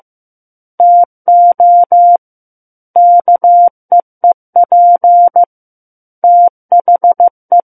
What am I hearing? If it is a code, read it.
Morse code, heard as a keyed tone, about 15 wpm, 700 Hz. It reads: TO KEEP THE
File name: Book One / 06-Chapter 6 - 15 WPM